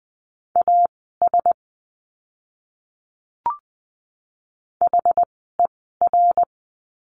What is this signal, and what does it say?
Morse code, heard as a keyed tone, about 20 wpm, spaced as for 15 wpm, 700 Hz. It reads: AS HER
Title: 230 Lesson 37f - Letter x - Words 20wpm